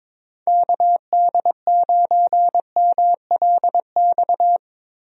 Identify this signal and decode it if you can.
Morse code, heard as a keyed tone, about 21 wpm, 700 Hz. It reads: KD9MLX
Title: USA Call Signs - 6 Characters - Repeated 3x - 22wpm